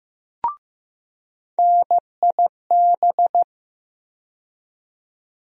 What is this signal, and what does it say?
Morse code, heard as a keyed tone, about 15 wpm, 700 Hz. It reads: NIB
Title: USA Call Sign Postfixes - 3 Letters 15wpm